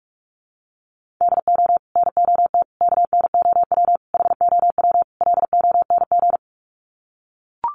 Morse code, heard as a keyed tone, about 44 wpm, 700 Hz. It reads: DO NOT KNOW HOW LONG
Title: Sets of 5 Words Encore2 45wpm